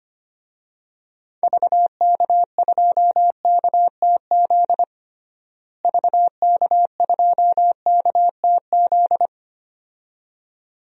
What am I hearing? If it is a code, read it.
Morse code, heard as a keyed tone, about 25 wpm, 700 Hz. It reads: VK2KTZ VK2KTZ
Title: DXCC-100 Call Signs - 6 Characters - Repeated 2x - 25wpm